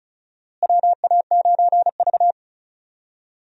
Morse code, heard as a keyed tone, about 36 wpm, 700 Hz. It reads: WA9V
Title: USA Call Signs - 4 Characters - Repeated 2x - 35wpm